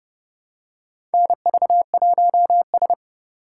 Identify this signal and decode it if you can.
Morse code, heard as a keyed tone, about 31 wpm, 700 Hz. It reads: NV1S
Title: USA Call Signs - 4 Characters - 2x Character Spacing - 30wpm